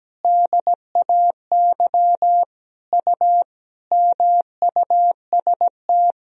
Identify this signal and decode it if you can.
Morse code, heard as a keyed tone, about 17 wpm, 700 Hz. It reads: DAY U MUST
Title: Sentences from Top 400 Words and Abbreviations 17wpm